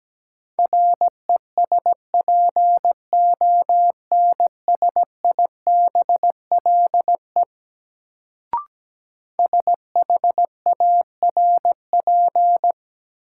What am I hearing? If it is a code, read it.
Morse code, heard as a keyed tone, about 17 wpm, 700 Hz. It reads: RESPONSIBLE SHARP
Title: Sentences from Top 1600 Words - New Word Drill Alt - 17wpm